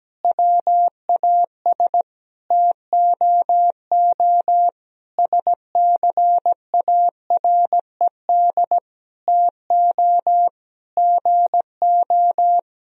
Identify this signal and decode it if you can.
Morse code, heard as a keyed tone, about 17 wpm, 700 Hz. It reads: WAS TOO SCARED TO GO
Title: Sentences from Top 1000 Words - Review Part 4 - 17wpm